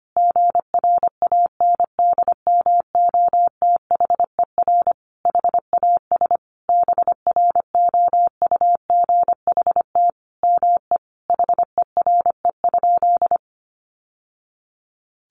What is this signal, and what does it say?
Morse code, heard as a keyed tone, about 25 wpm, 700 Hz. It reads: GRANDMOTHER HAS BROUGHT ME HERE?
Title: Sentences from Top 1200 Words 25wpm